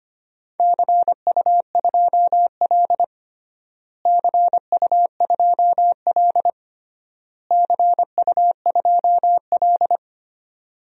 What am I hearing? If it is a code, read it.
Morse code, heard as a keyed tone, about 25 wpm, 700 Hz. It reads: CU2L CU2L CU2L
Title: DXCC-100 Call Signs - 4 Characters - Repeated 3x - 25wpm